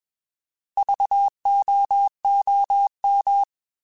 Morse code, heard as a keyed tone, about 21 wpm, 800 Hz. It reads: VOOM
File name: Alphabetical / Alphabetical mp3s / Vs @20